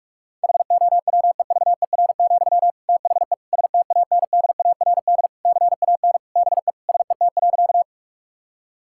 Morse code, heard as a keyed tone, about 46 wpm, 700 Hz. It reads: HOWEVER, THE STANDARD CAN BE SET.
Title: Sentences from Top 1000 Words - Review Part 6 - 45wpm